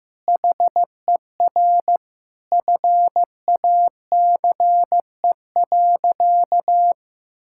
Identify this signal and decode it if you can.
Morse code, heard as a keyed tone, about 15 wpm, 700 Hz. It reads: HER FACE.
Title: Sentences from Top 1600 Words 15wpm